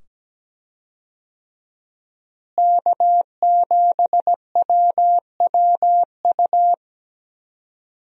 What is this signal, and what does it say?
Morse code, heard as a keyed tone, about 17 wpm, 700 Hz. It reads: K7WWU
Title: USA Call Signs - 5 Characters - Repeated 2x - 17wpm